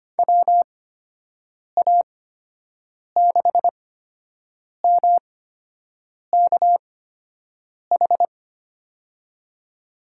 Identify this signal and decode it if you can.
Morse code, heard as a keyed tone, about 25 wpm, spaced as for 7 wpm, 700 Hz. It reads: WA6MKH
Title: USA Call Signs - 8x Character Spacing - 25wpm